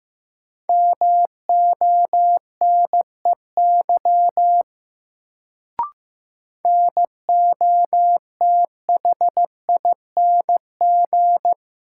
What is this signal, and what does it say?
Morse code, heard as a keyed tone, about 15 wpm, 700 Hz. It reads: MONEY NOTHING
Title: Sets of 4 Words - New Word Drill - 15wpm